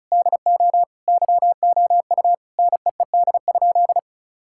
Morse code, heard as a keyed tone, about 34 wpm, 700 Hz. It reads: DO YOU NEED?
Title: Sentences from Top 1000 Words - Review Part 5 - 35wpm